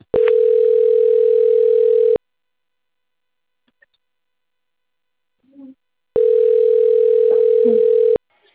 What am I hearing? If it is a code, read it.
Ringback tone (North America), 440 + 480 Hz.